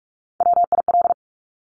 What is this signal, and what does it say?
Morse code, heard as a keyed tone, about 46 wpm, 700 Hz. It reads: WIL
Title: Top Three Letter Combinations in Words 45wpm